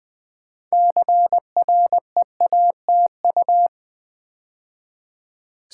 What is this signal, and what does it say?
Morse code, heard as a keyed tone, about 20 wpm, 700 Hz. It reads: CREATU
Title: Build A Word - 2 Letters At A Time - 8-Letter Words - 20wpm